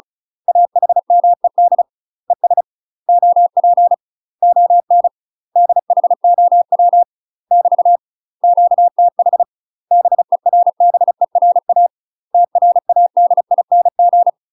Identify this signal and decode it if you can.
Morse code, heard as a keyed tone, about 34 wpm, 700 Hz. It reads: AHMED ES OP ON DHOW <BT> QTH BERBERA TRADING
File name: UK MASTER QSOs TRAINING/35WPM